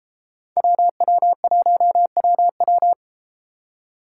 Morse code, heard as a keyed tone, about 33 wpm, 700 Hz. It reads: WW1WW